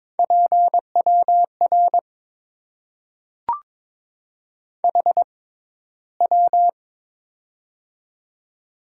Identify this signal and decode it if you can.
Morse code, heard as a keyed tone, about 22 wpm, spaced as for 14 wpm, 700 Hz. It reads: PWR HW
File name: Top 90 QSO Elements - 6x Character Spacing - 22wpm